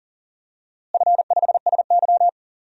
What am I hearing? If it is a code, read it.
Morse code, heard as a keyed tone, about 37 wpm, 700 Hz. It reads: F5SY
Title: DXCC-100 Call Signs - 4 Characters - Repeated 3x - 40wpm